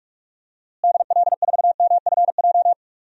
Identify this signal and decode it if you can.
Morse code, heard as a keyed tone, about 46 wpm, 700 Hz. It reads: DL4MFJ
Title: DXCC-100 Call Signs - 6 Characters - Repeated 3x - 45wpm